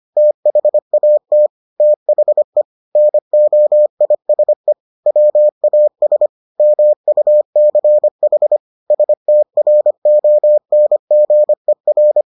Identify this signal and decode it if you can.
Morse code, heard as a keyed tone, about 25 wpm, 600 Hz. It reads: THAT THE NOISE WAS MUCH STRONGER